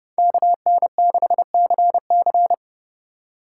Morse code, heard as a keyed tone, about 32 wpm, 700 Hz. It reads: KN6CC
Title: USA Call Signs - 5 Characters - 2x Character Spacing - 30wpm